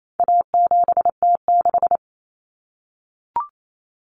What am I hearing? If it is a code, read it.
Morse code, heard as a keyed tone, about 28 wpm, 700 Hz. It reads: A7T6
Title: Common Cut Numbers - 4-Digit 28wpm